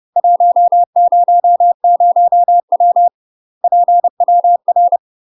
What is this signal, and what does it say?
Morse code, heard as a keyed tone, about 30 wpm, 700 Hz. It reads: 100W PWR
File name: UK MASTER QSOs TRAINING / 30WPM / UK MASTER QSOs 0009